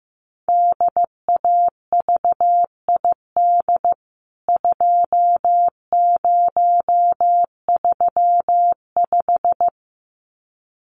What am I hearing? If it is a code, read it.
Morse code, heard as a keyed tone, about 15 wpm, 700 Hz. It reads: DAVID 2035
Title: CWT Exchange 15wpm